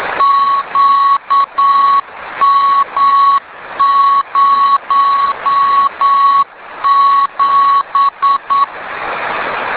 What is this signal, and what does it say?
Morse code, heard as a keyed tone, about 9 wpm, 1110 Hz. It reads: QM07